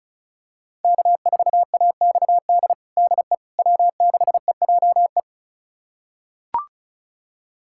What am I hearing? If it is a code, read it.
Morse code, heard as a keyed tone, about 36 wpm, 700 Hz. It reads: K4AXD DE W6EJE
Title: USA Callsign DE Callsign 35wpm